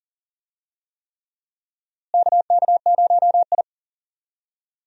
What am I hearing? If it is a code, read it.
Morse code, heard as a keyed tone, about 40 wpm, 700 Hz. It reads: KK0I
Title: USA Call Signs - 4 Characters - Repeated 3x - 40wpm